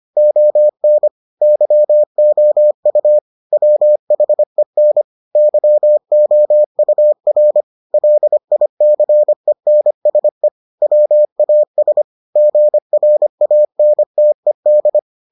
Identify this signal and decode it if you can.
Morse code, heard as a keyed tone, about 25 wpm, 600 Hz. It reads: ON YOU WHEN YOUR LICENSE WAS GRANTED